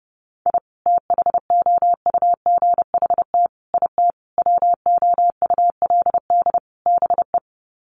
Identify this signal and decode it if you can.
Morse code, heard as a keyed tone, about 30 wpm, 700 Hz. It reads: I THOUGHT IT WOULD BE
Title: Sets of 5 Words Encore 30wpm